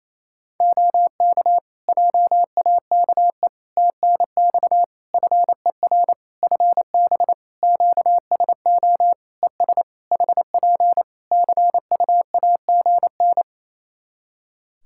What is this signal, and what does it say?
Morse code, heard as a keyed tone, about 28 wpm, 700 Hz. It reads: OK JAKE TNX FER FB QSO ES HP CUAGN
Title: Practice QSO - Exchange 4 - Ending - 28wpm